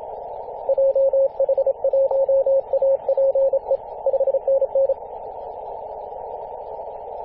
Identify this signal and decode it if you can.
Morse code, heard as a keyed tone, about 27 wpm, 540 Hz. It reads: JH1APE 5NN
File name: posts / 2016-arp-15-2115z-vk9l-g7vjr-18-081mhz-cw